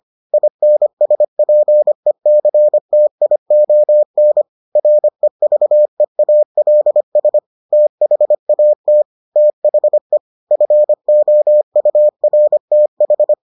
Morse code, heard as a keyed tone, about 25 wpm, 600 Hz. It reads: INSPECTION REVEALS THAT THE FOURTH